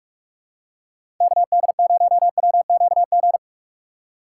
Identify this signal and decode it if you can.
Morse code, heard as a keyed tone, about 41 wpm, 700 Hz. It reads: KD0WQG
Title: USA Call Signs - 6x Character Spacing - 45wpm